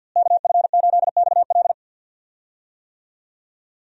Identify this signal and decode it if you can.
Morse code, heard as a keyed tone, about 50 wpm, 700 Hz. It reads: KF8XL
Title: USA Call Signs - 5 Characters - Repeated 2x - 50wpm